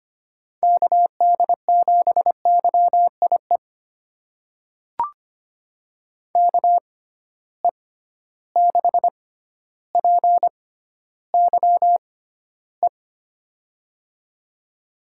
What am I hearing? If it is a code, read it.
Morse code, heard as a keyed tone, about 25 wpm, spaced as for 13 wpm, 700 Hz. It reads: KD7YIE KE6PYE
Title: USA Call Signs - 6 Characters - 6x Character Spacing - 25wpm